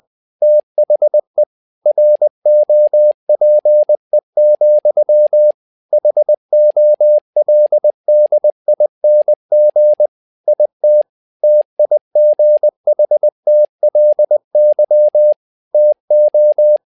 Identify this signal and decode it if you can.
Morse code, heard as a keyed tone, about 20 wpm, 600 Hz. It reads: THE ROPE, HOLDING IT TIGHTLY TO
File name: learning center / tower trouble / tower trouble 20wpm